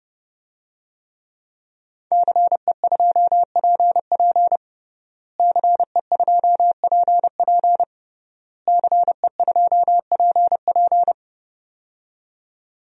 Morse code, heard as a keyed tone, about 32 wpm, 700 Hz. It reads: CE2PP CE2PP CE2PP
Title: DXCC-100 Call Signs - 5 Characters - Repeated 3x - 30wpm